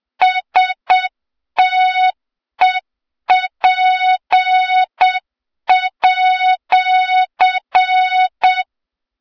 Morse code, heard as a keyed tone, about 7 wpm, 3700 Hz, with noise. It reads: STEP@
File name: Morze-HA2MN / hang / 23 GYAKORLAT